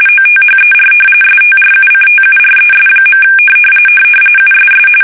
RTTY, about 45.45 baud. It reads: OTHER STRUCTURES. USS
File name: fsk850 / battleshipnj